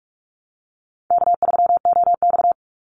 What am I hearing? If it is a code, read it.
Morse code, heard as a keyed tone, about 44 wpm, 700 Hz. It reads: K3OX
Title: Call Signs Repeated 2x - 45wpm